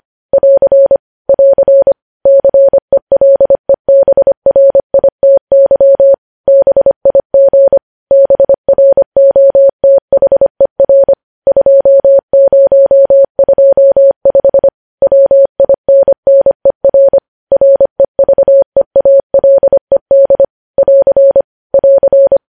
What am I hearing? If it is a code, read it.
Morse code, heard as a keyed tone, about 25 wpm, 550 Hz. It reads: <AR> <AR> CELEBRITY BIG BROTHER 2025 WINNER REVEALED <AR> <AR>
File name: podcast / News headlines-1x-25x25